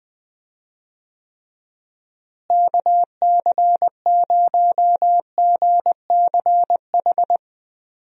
Morse code, heard as a keyed tone, about 20 wpm, 700 Hz. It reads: KC0GCH
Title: USA Call Signs - 6 Characters - Repeated 3x - 20wpm